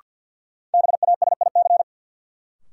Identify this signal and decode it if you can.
Morse code, heard as a keyed tone, about 52 wpm, 700 Hz. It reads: BASIC
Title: Sentences from Top 1000 Words - Review Part 3 - New Word Drill - 50wpm